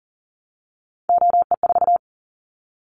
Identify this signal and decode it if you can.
Morse code, heard as a keyed tone, about 43 wpm, 700 Hz. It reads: OE4
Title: Callsign Prefixes for Top 50 DXCC - 40wpm